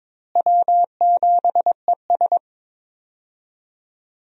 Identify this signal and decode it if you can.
Morse code, heard as a keyed tone, about 22 wpm, 700 Hz. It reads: W7ES
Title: USA Call Signs - 4 Characters - Repeated 2x - 22wpm